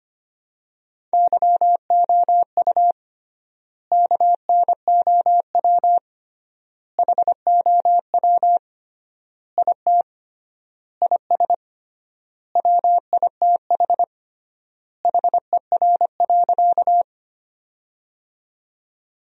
Morse code, heard as a keyed tone, about 25 wpm, 700 Hz. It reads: YOU KNOW HOW IT IS WITH HER.
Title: 3x Word-Spacing - Sentences from Top 100 Words - 25wpm